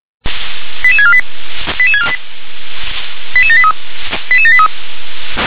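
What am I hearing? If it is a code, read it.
Five-tone selective call (ZVEI), heard as quick runs of five tones, tones of about 70 ms. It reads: 88648 88648 88783 88783
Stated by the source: ZVEI five-tone selective call 88634 received by SDR.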